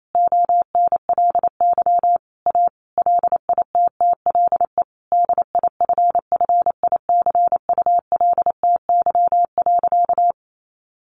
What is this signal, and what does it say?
Morse code, heard as a keyed tone, about 28 wpm, 700 Hz. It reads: ONLY A LITTLE DIFFICULTY.
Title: Sentences from Top 1500 Words 28wpm